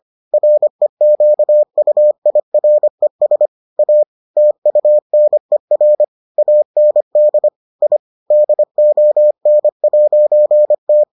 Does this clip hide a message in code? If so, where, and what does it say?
Yes: Morse code from 0.3 to 11.0 s, heard as a keyed tone, about 25 wpm, 600 Hz. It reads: REQUIRES A TUNER AND I DON'T